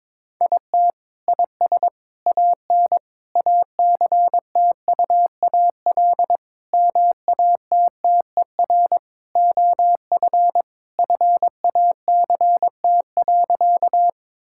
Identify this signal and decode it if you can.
Morse code, heard as a keyed tone, about 21 wpm, 700 Hz. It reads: IT IS AN ACTUAL MATTER OF FACT.
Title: Sentences from Top 1000 Words - Review Part 5 - 22wpm